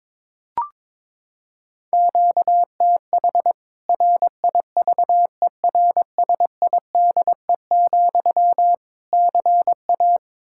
Morse code, heard as a keyed tone, about 22 wpm, 700 Hz. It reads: QTH RIVERSIDE, CA
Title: Practice QSO - Exchange 1 - Beginning - 22wpm